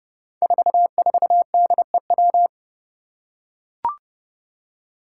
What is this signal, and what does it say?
Morse code, heard as a keyed tone, about 28 wpm, 700 Hz. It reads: 44DEW
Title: Five Challenging Letter-Number Combinations 30wpm